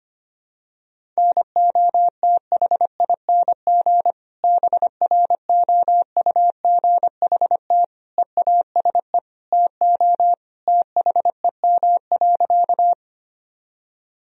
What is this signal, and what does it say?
Morse code, heard as a keyed tone, about 25 wpm, 700 Hz. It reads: NOTHING BROUGHT EASE TO THEM.
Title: Sentences from Top 1000 Words - Review Part 1 - 25wpm